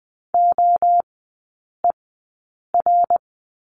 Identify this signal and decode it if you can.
Morse code, heard as a keyed tone, about 20 wpm, spaced as for 8 wpm, 700 Hz. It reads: OER
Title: 100 Lesson 17e - Letter u - Three Characters 20wpm